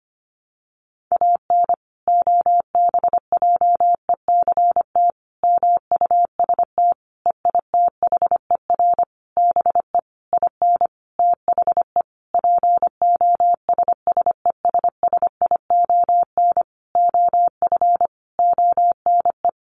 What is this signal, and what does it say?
Morse code, heard as a keyed tone, about 25 wpm, 700 Hz. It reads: AN OBJECT MUST EITHER BE IN THE POSSESSION OF ONE
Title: Sentences from Top 1400 Words 25wpm